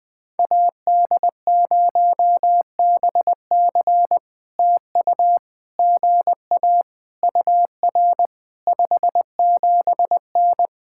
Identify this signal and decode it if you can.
Morse code, heard as a keyed tone, about 20 wpm, 700 Hz. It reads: AD0BC TU GA UR 57N
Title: POTA QSO Practice - 20wpm